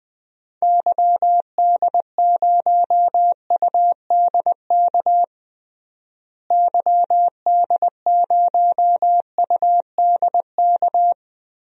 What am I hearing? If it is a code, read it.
Morse code, heard as a keyed tone, about 20 wpm, 700 Hz. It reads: YD0UDK YD0UDK
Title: DXCC-100 Call Signs - 6 Characters - Repeated 3x - 20wpm